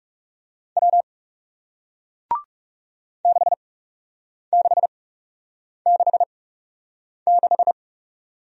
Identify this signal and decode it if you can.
Morse code, heard as a keyed tone, about 42 wpm, spaced as for 8 wpm, 700 Hz. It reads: W 6666